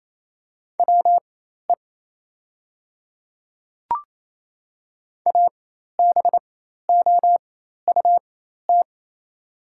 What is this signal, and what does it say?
Morse code, heard as a keyed tone, about 28 wpm, spaced as for 13 wpm, 700 Hz. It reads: WE ABOUT
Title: 4x Character Spacing - Top 100 Words - 28wpm